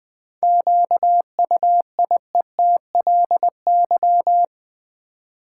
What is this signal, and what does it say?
Morse code, heard as a keyed tone, about 20 wpm, 700 Hz. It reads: QUIETLY
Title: 7 Letter Words - First 3 Letters Given - 20wpm